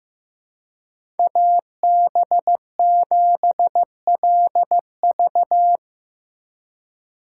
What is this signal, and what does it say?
Morse code, heard as a keyed tone, about 15 wpm, 700 Hz. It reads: AB7LV
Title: USA Call Signs - 5 Characters - 4x Character Spacing - 15wpm